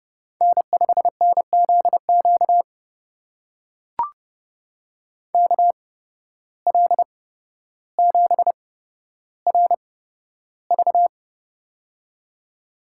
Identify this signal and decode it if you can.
Morse code, heard as a keyed tone, about 30 wpm, spaced as for 13 wpm, 700 Hz. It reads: N5NZQ KL7RV
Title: USA Call Signs - 5 Characters - 8x Character Spacing - 30wpm